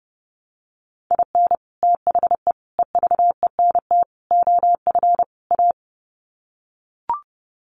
Morse code, heard as a keyed tone, about 30 wpm, 700 Hz. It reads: IN THE EVENT OF A
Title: Sets of 5 Words Encore 30wpm